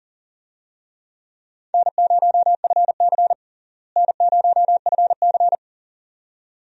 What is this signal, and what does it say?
Morse code, heard as a keyed tone, about 40 wpm, 700 Hz. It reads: N0FC N0FC
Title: USA Call Signs - 4 Characters - Repeated 2x - 40wpm